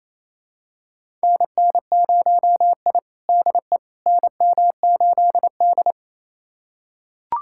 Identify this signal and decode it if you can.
Morse code, heard as a keyed tone, about 28 wpm, 700 Hz. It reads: NN0I DE NM8D
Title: USA Callsign DE Callsign - 4 Characters - 28wpm